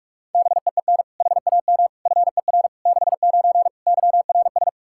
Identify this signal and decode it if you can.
Morse code, heard as a keyed tone, about 46 wpm, 700 Hz. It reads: BEEN HAM FER 69 YRS
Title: Practice QSO - Exchange 3 - More Info - 45wpm